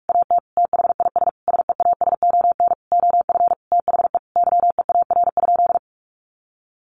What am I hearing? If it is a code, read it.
Morse code, heard as a keyed tone, about 44 wpm, 700 Hz. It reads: AT THIS SEASON OF THE YEAR?
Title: Sentences from Top 1400 Words 45wpm